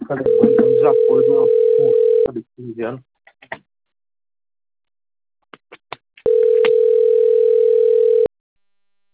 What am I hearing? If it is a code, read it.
Ringback tone (North America), 440 + 480 Hz, with speech.
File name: rg-1600-unknown- - - .69610